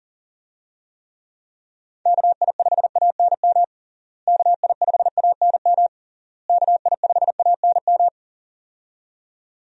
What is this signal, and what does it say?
Morse code, heard as a keyed tone, about 40 wpm, 700 Hz. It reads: KI5ANM KI5ANM KI5ANM
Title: USA Call Signs - 6 Characters - Repeated 3x - 40wpm